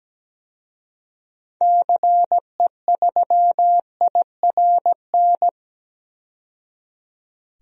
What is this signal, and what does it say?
Morse code, heard as a keyed tone, about 17 wpm, 700 Hz. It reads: CE3IRN
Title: DXCC-100 Call Signs - 6 Characters - 17wpm